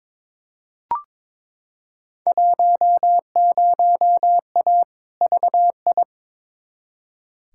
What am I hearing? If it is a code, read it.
Morse code, heard as a keyed tone, about 22 wpm, 700 Hz. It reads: 10A VI
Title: Field Day Exchanges Encore - 22wpm